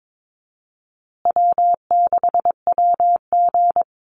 Morse code, heard as a keyed tone, about 22 wpm, 700 Hz. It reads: W6WG
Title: Easy Call Signs Repeated 3x - 22wpm